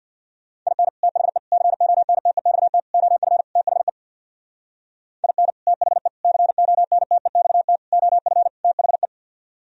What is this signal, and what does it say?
Morse code, heard as a keyed tone, about 64 wpm, 700 Hz. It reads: IN THE CONTEXT OF THE IN THE CONTEXT OF THE